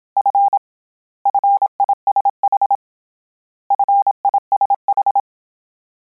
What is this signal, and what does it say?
Morse code, heard as a keyed tone, about 27 wpm, 800 Hz. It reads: F FISH FISH
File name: Combinations / Combinations-mp3s / Combinations@25